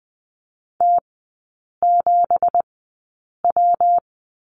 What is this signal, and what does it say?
Morse code, heard as a keyed tone, about 20 wpm, spaced as for 8 wpm, 690 Hz. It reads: T7W
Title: 166 Lesson 27e - Number 7 - Three Characters 20wpm